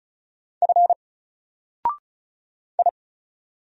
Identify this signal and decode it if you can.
Morse code, heard as a keyed tone, about 36 wpm, 700 Hz. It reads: F I
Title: Single Character (Letter:Number:Punctuation:Prosign) 35wpm